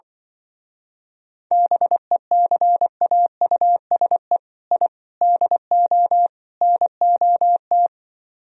Morse code, heard as a keyed tone, about 24 wpm, 700 Hz. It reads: BECAUSE I DO NOT
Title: Speed-Racing - Sets of 4 Words 20wpm